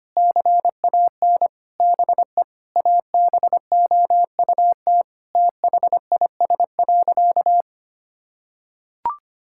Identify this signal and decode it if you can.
Morse code, heard as a keyed tone, about 25 wpm, 700 Hz. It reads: CAN BE ABOUT THIS.
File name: practice / 25wpm / Sentences with 9-Word Phrases - 25wpm